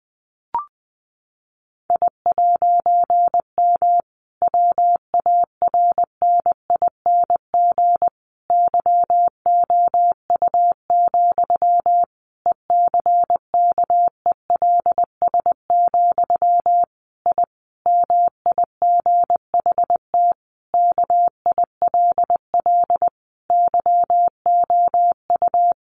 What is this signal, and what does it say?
Morse code, heard as a keyed tone, about 20 wpm, 700 Hz. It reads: I'M WARNING YOU, ECKELS, I MIGHT KILL YOU
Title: A Sound Of Thunder 20wpm